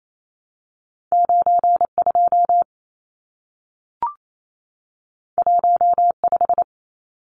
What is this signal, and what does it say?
Morse code, heard as a keyed tone, about 28 wpm, 700 Hz. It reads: 92 15
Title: Numbers - 2-Digit 28wpm